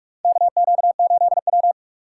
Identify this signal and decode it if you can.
Morse code, heard as a keyed tone, about 41 wpm, 700 Hz. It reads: KQ8W
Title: USA Call Signs - 4 Characters - 8x Character Spacing - 45wpm